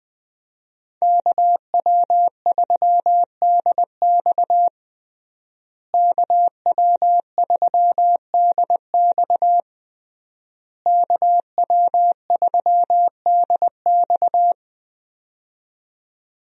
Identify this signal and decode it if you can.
Morse code, heard as a keyed tone, about 20 wpm, 700 Hz. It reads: KW3DX KW3DX KW3DX
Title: USA Call Signs - 5 Characters - Repeated 3x - 20wpm